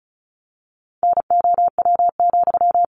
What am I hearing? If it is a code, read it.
Morse code, heard as a keyed tone, about 34 wpm, 700 Hz. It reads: NOW,
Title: Sentences from Top 400 Words 35wpm